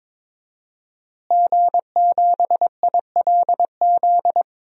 Morse code, heard as a keyed tone, about 21 wpm, 700 Hz. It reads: G7ILZ
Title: DXCC-100 Call Signs - 5 Characters - Repeated 3x - 22wpm